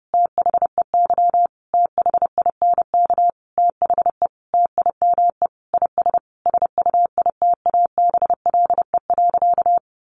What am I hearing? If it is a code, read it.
Morse code, heard as a keyed tone, about 30 wpm, 700 Hz. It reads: THEY THINK THE TIME IS SUITABLE.
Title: Sentences from Top 1400 Words 30wpm